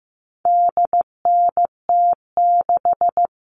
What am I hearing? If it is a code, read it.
Morse code, heard as a keyed tone, about 15 wpm, 700 Hz. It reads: DNT6
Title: All Cut Numbers - 4-Digit 15wpm